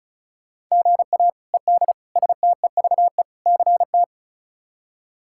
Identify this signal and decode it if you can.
Morse code, heard as a keyed tone, about 36 wpm, 700 Hz. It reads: GA ED STEVE CT
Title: K1USN SST QSO Practice - 35wpm